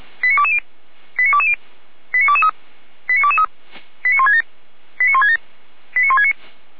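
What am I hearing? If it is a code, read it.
Five-tone selective call (ZVEI), heard as quick runs of five tones, tones of about 80 ms. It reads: 89220 89220 89223 89223 89167 89167 89169
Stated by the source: Received on an SDR as a ZVEI five-tone call.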